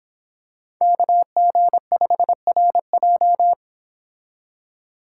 Morse code, heard as a keyed tone, about 26 wpm, 700 Hz. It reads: KG5RJ